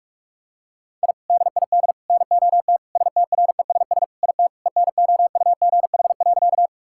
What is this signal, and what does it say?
Morse code, heard as a keyed tone, about 45 wpm, 700 Hz. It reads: I DID NOT STRESS IT ENOUGH.
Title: Sentences from Top 1000 Words - Review Part 6 - 45wpm